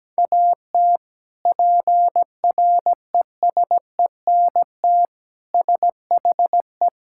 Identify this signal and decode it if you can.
Morse code, heard as a keyed tone, about 17 wpm, 700 Hz. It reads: AT PRESENT SHE
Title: Sentences from Top 1000 Words - Review Part 5 - 17wpm